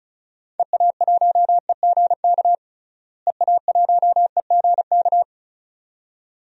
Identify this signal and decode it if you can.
Morse code, heard as a keyed tone, about 36 wpm, 700 Hz. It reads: EA1EGK EA1EGK
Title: DXCC-100 Call Signs - 6 Characters - Repeated 2x - 35wpm